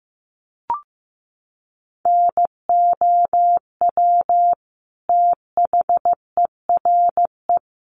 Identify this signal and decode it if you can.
Morse code, heard as a keyed tone, about 14 wpm, 700 Hz. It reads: NOW THERE
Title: Sentences from Top 100 Words 15wpm